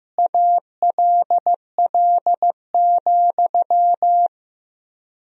Morse code, heard as a keyed tone, about 15 wpm, 700 Hz. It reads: ALL,
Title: 3x Word-Spacing - Sentences from Top 500 Words - 15wpm